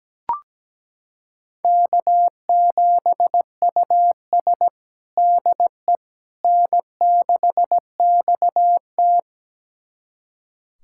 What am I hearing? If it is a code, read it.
Morse code, heard as a keyed tone, about 17 wpm, 700 Hz. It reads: K7US DE N6XT
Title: USA Callsign DE Callsign - 4 Characters - 17wpm